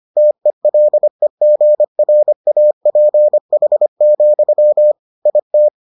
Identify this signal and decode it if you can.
Morse code, heard as a keyed tone, about 25 wpm, 600 Hz. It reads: TELEGRAPH, IT